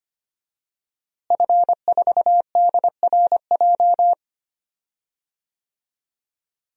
Morse code, heard as a keyed tone, about 25 wpm, 700 Hz. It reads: F4DRJ
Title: DXCC-100 Call Signs - 5 Characters - Repeated 3x - 25wpm